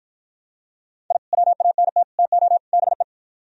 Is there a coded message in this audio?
Yes: Morse code from 1.1 to 3.0 s, heard as a keyed tone, about 52 wpm, 700 Hz. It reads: I WANT TO BE